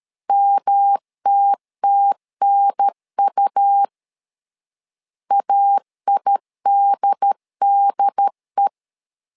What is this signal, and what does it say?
Morse code, heard as a keyed tone, about 13 wpm, 800 Hz. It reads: MTTNU AIDDE
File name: LEZIONI / 3 MWRO / 3 M 12wpm 1541